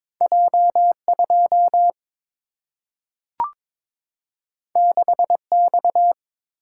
Morse code, heard as a keyed tone, about 22 wpm, 700 Hz. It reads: J 2 6 X
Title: Single-Letter Single-Number 22wpm